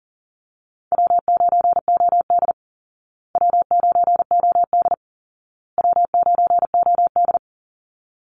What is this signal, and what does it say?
Morse code, heard as a keyed tone, about 37 wpm, 700 Hz. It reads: W9OD W9OD W9OD
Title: Easy Call Signs Repeated 3x - 40wpm